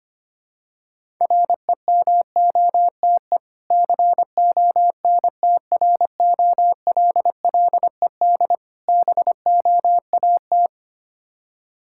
Morse code, heard as a keyed tone, about 25 wpm, 700 Hz. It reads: REMOTE CONTROLLED BOAT
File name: practice / 25wpm / Difficult Words with Hints - 25wpm